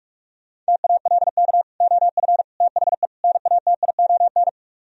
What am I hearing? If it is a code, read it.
Morse code, heard as a keyed tone, about 66 wpm, spaced as for 55 wpm, 700 Hz. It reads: TALK OF THE NATION
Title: Speed-Racing - Sets of 4 Words 45wpm